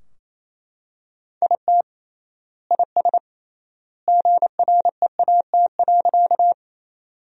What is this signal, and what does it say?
Morse code, heard as a keyed tone, about 28 wpm, 700 Hz. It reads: IT IS GREAT.
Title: 3x Word-Spacing - Sentences from Top 200 Words - 28wpm